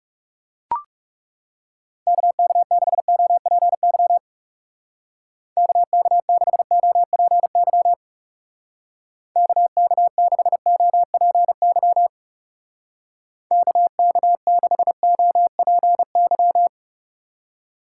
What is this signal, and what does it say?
Morse code, heard as a keyed tone, about 37 wpm, 700 Hz. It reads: KK6OPY KK6OPY KK6OPY KK6OPY
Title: Speed Racing - USA 2x3 Call Signs 30wpm